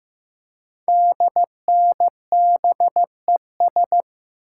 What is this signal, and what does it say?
Morse code, heard as a keyed tone, about 15 wpm, 700 Hz. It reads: DNBES
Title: Code Groups - 5 Letters - 15wpm